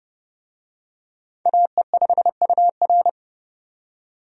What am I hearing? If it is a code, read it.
Morse code, heard as a keyed tone, about 30 wpm, 700 Hz. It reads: AE5UR
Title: USA Call Signs - 5 Characters - 30wpm